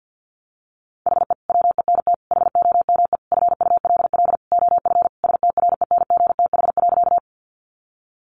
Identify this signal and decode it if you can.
Morse code, heard as a keyed tone, about 50 wpm, 700 Hz. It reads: HE WENT HOME FULL OF STRENGTH.
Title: Sentences from Top 1300 Words 50wpm